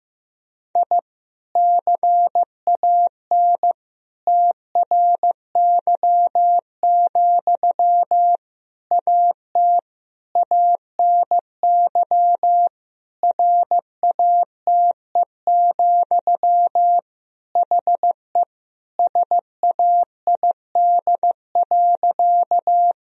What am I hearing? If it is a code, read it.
Morse code, heard as a keyed tone, about 15 wpm, 700 Hz. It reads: I CAN TRY, AT ANY RATE, HE SAID.
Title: Sentences from Top 1000 Words - Review Part 4 - 15wpm